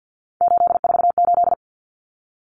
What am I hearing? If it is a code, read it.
Morse code, heard as a keyed tone, about 48 wpm, 700 Hz. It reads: 848
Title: Common Cut Numbers - 3-Digit 50wpm